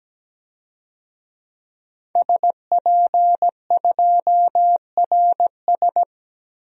Morse code, heard as a keyed tone, about 17 wpm, 700 Hz. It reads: SP2RS
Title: DXCC-100 Call Signs - 5 Characters - 17wpm